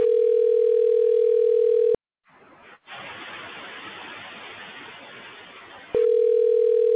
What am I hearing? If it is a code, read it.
Ringback tone (North America), 440 + 480 Hz.